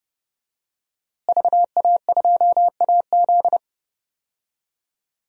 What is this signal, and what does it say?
Morse code, heard as a keyed tone, about 30 wpm, 700 Hz. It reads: VA2AZ
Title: DXCC-100 Call Signs - 5 Characters - Repeated 3x - 30wpm